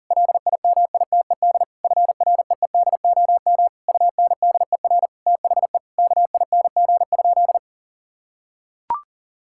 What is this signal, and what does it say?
Morse code, heard as a keyed tone, about 40 wpm, 700 Hz. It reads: LIMITED FREEDOM UNDER THE KING?
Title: Sentences from Top 1000 Words - Review Part 6 - 40wpm